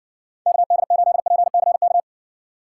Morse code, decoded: KD8PCK

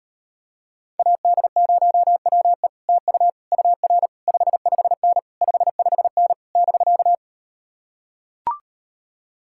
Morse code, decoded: AD0WE TU UR 55N 55N <BK>

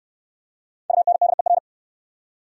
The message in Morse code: UNDER

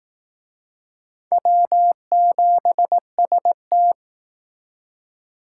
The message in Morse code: W7ST

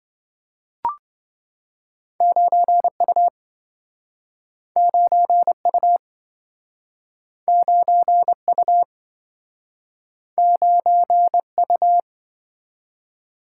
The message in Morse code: 9U 9U 9U 9U

